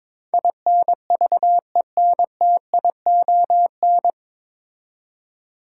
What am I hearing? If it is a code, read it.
Morse code, heard as a keyed tone, about 22 wpm, 700 Hz. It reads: INVENTION
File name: 9 Letter Words - First 3 Letters Given - 22wpm